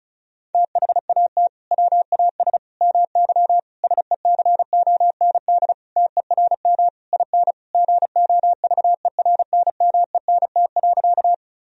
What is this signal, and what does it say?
Morse code, heard as a keyed tone, about 36 wpm, 700 Hz. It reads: THAT WAS MY SECOND TERM IN GOVERNMENT.
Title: Sentences from Top 1000 Words - Review Part 4 - 35wpm